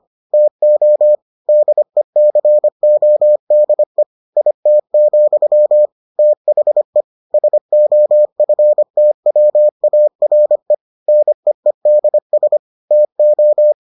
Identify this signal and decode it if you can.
Morse code, heard as a keyed tone, about 25 wpm, 600 Hz. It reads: TO DECODE IT, THE SOFTWARE NEEDS TO